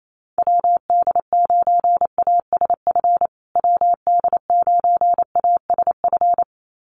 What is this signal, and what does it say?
Morse code, heard as a keyed tone, about 28 wpm, 700 Hz. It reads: WD9ASF WD9ASF
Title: USA Call Signs Repeated 2x 28wpm